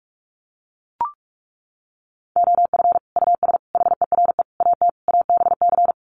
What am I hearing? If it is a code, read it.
Morse code, heard as a keyed tone, about 44 wpm, 700 Hz. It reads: OF US HERE AT ABC
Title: Sets of 5 Words 45wpm